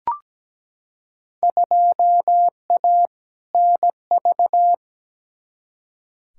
Morse code, heard as a keyed tone, about 17 wpm, 700 Hz. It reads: 2A NV
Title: Field Day Exchanges - 17wpm